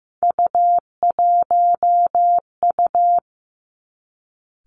Morse code, heard as a keyed tone, about 15 wpm, 700 Hz. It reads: U1U
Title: Three Challenging Letter-Number Combinations 15wpm